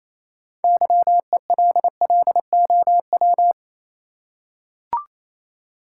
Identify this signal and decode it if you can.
Morse code, heard as a keyed tone, about 28 wpm, 700 Hz. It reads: YELLOW
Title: Six Letter Words - 28wpm